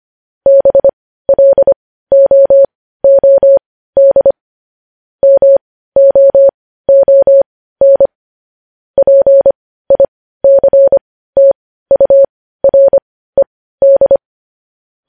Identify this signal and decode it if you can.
Morse code, heard as a keyed tone, about 25 wpm, spaced as for 15 wpm, 550 Hz. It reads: BLOOD MOON PICTURED
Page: news headlines in morse code at 15 wpm